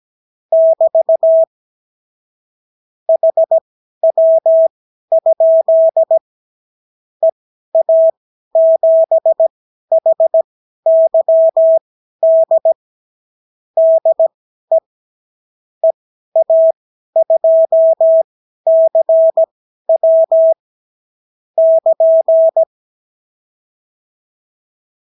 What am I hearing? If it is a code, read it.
Morse code, heard as a keyed tone, about 17 wpm, spaced as for 12 wpm, 650 Hz. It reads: <BT> HW? EA7HYD DE EA2CW <KN>
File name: CW QSO libroCW4 18-8ppm